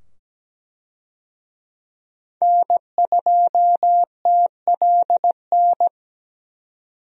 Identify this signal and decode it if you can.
Morse code, heard as a keyed tone, about 17 wpm, 700 Hz. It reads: N2TLN
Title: USA Call Signs - 5 Characters - 17wpm